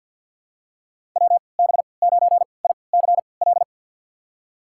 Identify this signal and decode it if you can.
Morse code, heard as a keyed tone, about 47 wpm, spaced as for 28 wpm, 700 Hz. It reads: WB9ICL
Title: USA Call Signs - 6 Characters - 3x Character Spacing - 50wpm